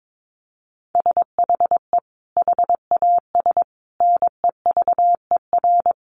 Morse code, heard as a keyed tone, about 22 wpm, 700 Hz. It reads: SHE HAS NEVER
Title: Additional Easy Phrases 22wpm